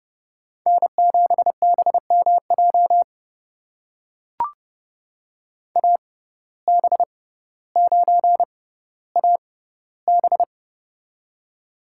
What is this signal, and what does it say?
Morse code, heard as a keyed tone, about 30 wpm, spaced as for 15 wpm, 700 Hz. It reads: N7BMJ AB9AB
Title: USA Call Signs - 5 Characters - 6x Character Spacing - 30wpm